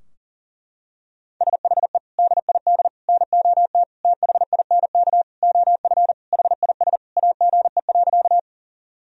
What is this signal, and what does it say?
Morse code, heard as a keyed tone, about 40 wpm, 700 Hz. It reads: SHE DID NOT THINK OF HIS AGE.
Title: Sentences from Top 1000 Words - Review Part 3 - 40wpm